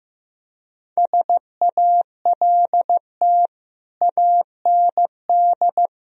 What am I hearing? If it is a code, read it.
Morse code, heard as a keyed tone, about 15 wpm, 700 Hz. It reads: SALT AND
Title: Speed-Racing - Sets of 5 Words 15wpm